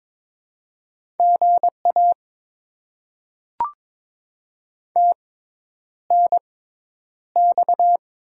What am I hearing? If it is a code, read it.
Morse code, heard as a keyed tone, about 22 wpm, spaced as for 10 wpm, 700 Hz. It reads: GA TNX
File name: Top 20 QSO Elements - 6x Character Spacing - 22wpm